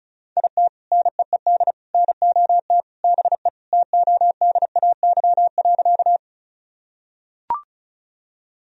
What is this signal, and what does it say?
Morse code, heard as a keyed tone, about 36 wpm, 700 Hz. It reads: IT NEED NOT BE TODAY.
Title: Sentences from Top 1000 Words - Review Part 4 - 35wpm